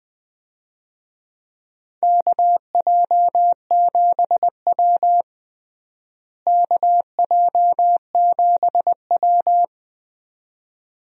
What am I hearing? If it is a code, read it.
Morse code, heard as a keyed tone, about 20 wpm, 700 Hz. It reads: KJ7W KJ7W
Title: USA Call Signs - 4 Characters - Repeated 2x - 20wpm